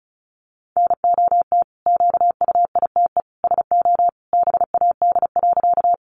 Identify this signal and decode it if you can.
Morse code, heard as a keyed tone, about 34 wpm, 700 Hz. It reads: NOT QUITE SO BAD.
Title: Sentences from Top 700 Words 35wpm